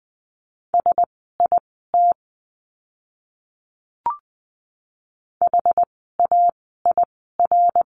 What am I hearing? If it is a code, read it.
Morse code, heard as a keyed tone, about 20 wpm, spaced as for 14 wpm, 700 Hz. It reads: SIT HAIR
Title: 101 Lesson 17f - Letter u - Words 20wpm